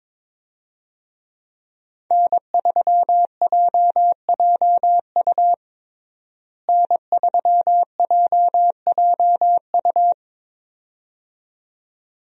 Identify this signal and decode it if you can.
Morse code, heard as a keyed tone, about 22 wpm, 700 Hz. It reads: N3JJU N3JJU
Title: USA Call Signs - 5 Characters - Repeated 2x - 22wpm